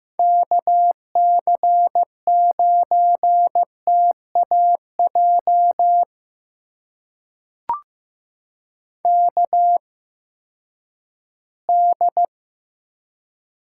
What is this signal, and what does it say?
Morse code, heard as a keyed tone, about 15 wpm, spaced as for 10 wpm, 700 Hz. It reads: KC9TAJ KD
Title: USA Call Signs - 6 Characters - 8x Character Spacing - 15wpm